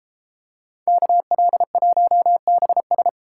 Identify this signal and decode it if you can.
Morse code, heard as a keyed tone, about 32 wpm, 700 Hz. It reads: KL1BS